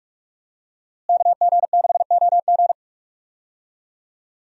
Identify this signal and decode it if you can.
Morse code, heard as a keyed tone, about 46 wpm, 700 Hz. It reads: KG6OG
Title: USA Call Signs - 5 Characters - Repeated 2x - 45wpm